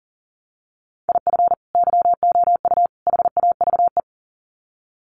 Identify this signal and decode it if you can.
Morse code, heard as a keyed tone, about 40 wpm, 700 Hz. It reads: IF YOU HAVE
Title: Sets of 3 Words 40wpm